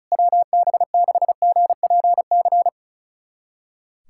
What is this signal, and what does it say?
Morse code, heard as a keyed tone, about 34 wpm, 700 Hz. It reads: WB6GPC
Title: USA Callsign DE Callsign - 6 Characters - 35wpm